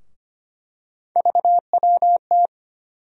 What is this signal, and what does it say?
Morse code, heard as a keyed tone, about 25 wpm, 700 Hz. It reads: VWT
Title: USA Call Sign Postfixes - 3 Letters 25wpm